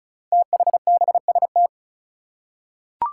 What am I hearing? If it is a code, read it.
Morse code, heard as a keyed tone, about 36 wpm, 700 Hz. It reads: THBST
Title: Code Groups - 5 Letters - 35wpm